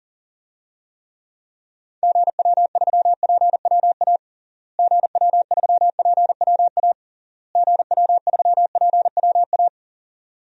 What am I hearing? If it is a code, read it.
Morse code, heard as a keyed tone, about 40 wpm, 700 Hz. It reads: GW3PWA GW3PWA GW3PWA
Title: DXCC-100 Call Signs - 6 Characters - Repeated 3x - 40wpm